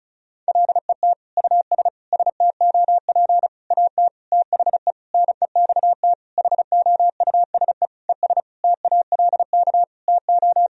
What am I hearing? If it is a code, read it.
Morse code, heard as a keyed tone, about 36 wpm, 700 Hz. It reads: LET US STOP AT THE NEXT HOUSE ES TALK TO
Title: Sentences from Top 400 Words and Abbreviations 35wpm